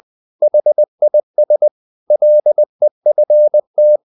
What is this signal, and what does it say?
Morse code, heard as a keyed tone, about 20 wpm, 600 Hz. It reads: HIS LEFT
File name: the old man and the sea / the old man and the sea part 1 20wpm